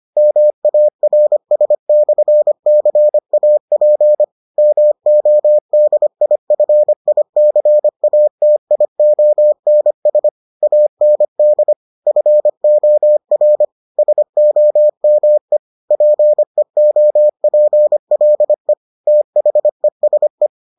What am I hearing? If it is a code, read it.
Morse code, heard as a keyed tone, about 25 wpm, 600 Hz. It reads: MARS/CAP MODIFICATIONS AND FOR SOME PEOPLE THESE